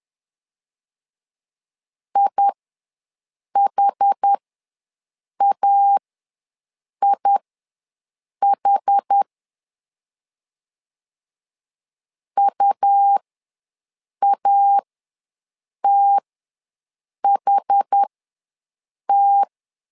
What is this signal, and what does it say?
Morse code, heard as a keyed tone, about 10 wpm, spaced as for 6 wpm, 800 Hz. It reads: IHAIH UATHT
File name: LEZIONI / 2 UHDN / 2 UH ws 10wpm 2029